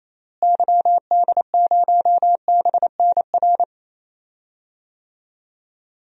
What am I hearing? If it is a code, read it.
Morse code, heard as a keyed tone, about 28 wpm, 700 Hz. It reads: YD0BNR
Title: DXCC-100 Call Signs - 6 Characters - Repeated 3x - 28wpm